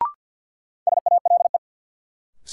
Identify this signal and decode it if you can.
Morse code, heard as a keyed tone, about 52 wpm, 700 Hz. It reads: SALE
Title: Sentences from Top 1400 Words - New Word Drill - 50wpm